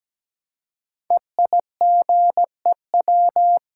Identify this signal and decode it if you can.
Morse code, heard as a keyed tone, about 17 wpm, 700 Hz. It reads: EIGEW
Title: Code Groups - 5 Letters - 17wpm